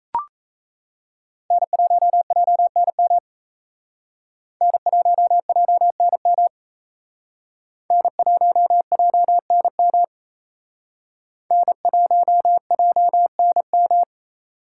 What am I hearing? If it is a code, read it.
Morse code, heard as a keyed tone, about 35 wpm, 700 Hz. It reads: N1JNM N1JNM N1JNM N1JNM
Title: Speed Racing - USA Call Signs 28wpm